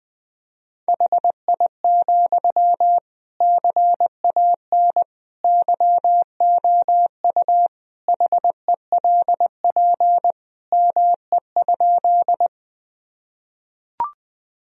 Morse code, heard as a keyed tone, about 20 wpm, 700 Hz. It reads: HI, CAN YOU HELP ME?
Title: Sentences from Top 1000 Words - Review Part 3 - 20wpm